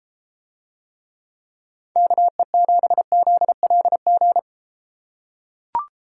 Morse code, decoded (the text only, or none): KE7ZLG